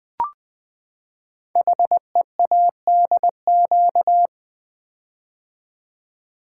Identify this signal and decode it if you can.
Morse code, heard as a keyed tone, about 20 wpm, 700 Hz. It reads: HEADQ
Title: Build A Word - 5 Letters At A Time - 12-Letter Words - 20wpm